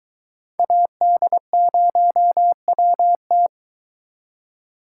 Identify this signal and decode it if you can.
Morse code, heard as a keyed tone, about 22 wpm, 700 Hz. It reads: AD0WT